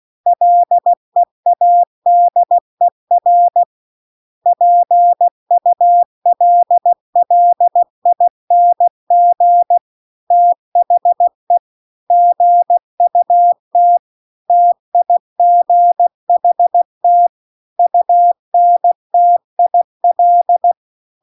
Morse code, heard as a keyed tone, about 16 wpm, 700 Hz. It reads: LEADER PULLING THE GUT TIGHT UNTIL
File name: CW-The-Big-Twoheaded-River-2-16WPM